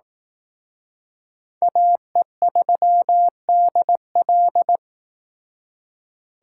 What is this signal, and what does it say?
Morse code, heard as a keyed tone, about 18 wpm, 700 Hz. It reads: AE3DL
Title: Speed Racing - USA 2x2 Call Signs 15wpm